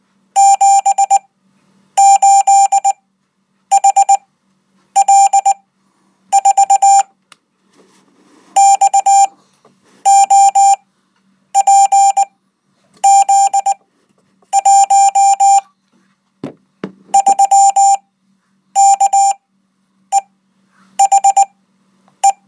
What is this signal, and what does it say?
Morse code, heard as a keyed tone, about 19 wpm, spaced as for 8 wpm, 790 Hz. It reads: 78HL4 XOPZ1 3KEHE